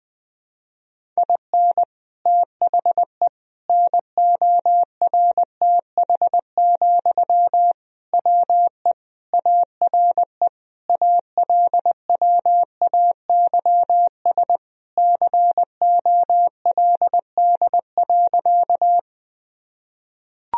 Morse code, heard as a keyed tone, about 20 wpm, 700 Hz. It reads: IN THE NORTH, WE ARE ALWAYS COLD.
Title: Sentences from Top 1000 Words - Review Part 1 - 20wpm